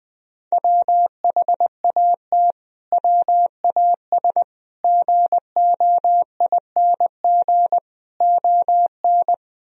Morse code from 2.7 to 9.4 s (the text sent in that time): WAS GOING ON